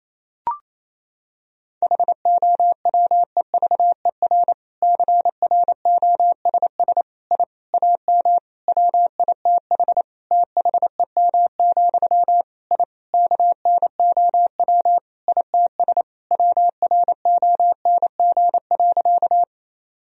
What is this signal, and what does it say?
Morse code, heard as a keyed tone, about 28 wpm, 700 Hz. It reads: HOWEVER CROSS I AM WITH THEM, I KNOW ITS WRONG.
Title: Sentences from Top 1000 Words - Review Part 1 - 28wpm